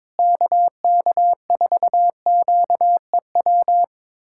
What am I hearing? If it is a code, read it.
Morse code, heard as a keyed tone, about 22 wpm, 700 Hz. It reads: KK4QEW
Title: USA Callsign DE Callsign 22wpm